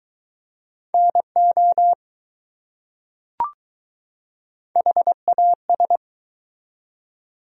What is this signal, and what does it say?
Morse code, heard as a keyed tone, about 21 wpm, 700 Hz. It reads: NO HAS